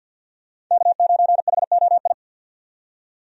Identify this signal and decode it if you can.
Morse code, heard as a keyed tone, about 51 wpm, 700 Hz. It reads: K9HOI
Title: USA Call Signs - 5 Characters - 50wpm